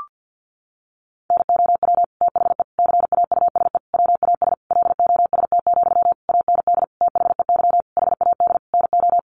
Morse code, heard as a keyed tone, about 50 wpm, 700 Hz. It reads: NOW THE CAUSE WAS LOST, AND THEY HAD NO
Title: Sentences from Top 1500 Words 50wpm